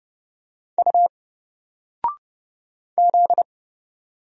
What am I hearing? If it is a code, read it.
Morse code, heard as a keyed tone, about 30 wpm, 700 Hz. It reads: U Z
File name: Single-Letter-Number-30wpm